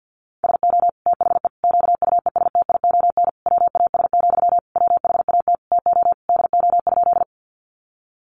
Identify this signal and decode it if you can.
Morse code, heard as a keyed tone, about 50 wpm, 700 Hz. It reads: SO THE QUESTION WAS, WHAT TO DO?
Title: Sentences from Top 400 Words 50wpm